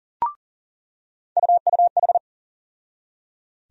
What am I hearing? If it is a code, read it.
Morse code, heard as a keyed tone, about 39 wpm, 700 Hz. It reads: UUH
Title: USA Call Sign Postfixes - 3 Letters 40wpm